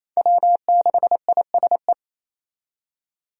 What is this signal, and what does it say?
Morse code, heard as a keyed tone, about 28 wpm, 700 Hz. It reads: W6ISE
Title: USA Call Signs - 5 Characters - 28wpm